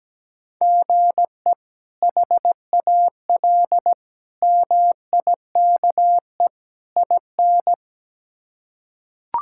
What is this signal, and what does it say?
Morse code, heard as a keyed tone, about 17 wpm, 700 Hz. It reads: GE HAL MIKE IN
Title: K1USN SST QSO Practice - 17wpm